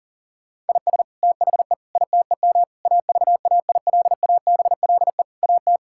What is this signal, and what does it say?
Morse code, heard as a keyed tone, about 37 wpm, 700 Hz. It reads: IS THE ITEM AVAILABLE AT
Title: Sentences from Top 1000 Words - Review Part 6 - 40wpm